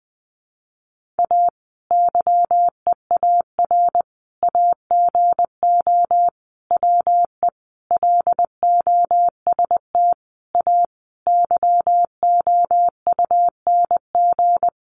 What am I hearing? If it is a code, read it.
Morse code, heard as a keyed tone, about 20 wpm, 700 Hz. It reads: A YEAR AGO WE LOST A YOUNG
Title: Sentences from Top 1400 Words 20wpm